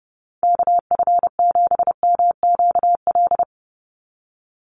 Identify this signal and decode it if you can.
Morse code, heard as a keyed tone, about 30 wpm, 700 Hz. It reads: KF7MQL
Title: USA Call Signs 30wpm